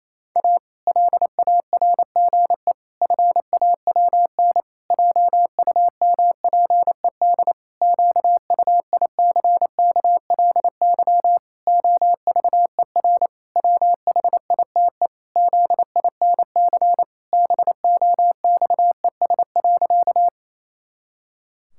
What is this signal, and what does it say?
Morse code, heard as a keyed tone, about 28 wpm, 700 Hz. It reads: A LARGE FAWN JUMPED QUICKLY OVER WHITE ZINC BOXES.
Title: Pangram Challenge v2 - 28wpm